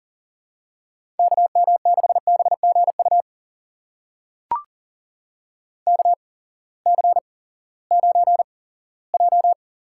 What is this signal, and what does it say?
Morse code, heard as a keyed tone, about 40 wpm, spaced as for 20 wpm, 700 Hz. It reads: KM6BGU KC9J
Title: USA Call Signs - 6 Characters - 8x Character Spacing - 40wpm